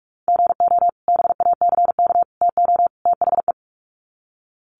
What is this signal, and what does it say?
Morse code, heard as a keyed tone, about 45 wpm, 700 Hz. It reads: GO BACK TO THE
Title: Sets of 4 Words 45wpm